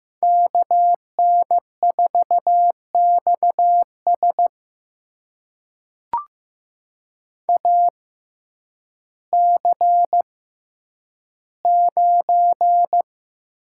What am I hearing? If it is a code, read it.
Morse code, heard as a keyed tone, about 15 wpm, spaced as for 9 wpm, 700 Hz. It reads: KN4XS AC9
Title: USA Call Signs - 5 Characters - 6x Character Spacing - 15wpm